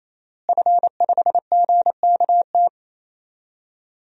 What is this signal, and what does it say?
Morse code, heard as a keyed tone, about 27 wpm, 700 Hz. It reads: F5GKT
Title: DXCC-100 Call Signs - 5 Characters - Repeated 3x - 28wpm